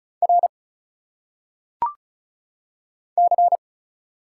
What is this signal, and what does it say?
Morse code, heard as a keyed tone, about 34 wpm, 700 Hz. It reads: R C
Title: Single Letters - Phonetic Pronunciation 35wpm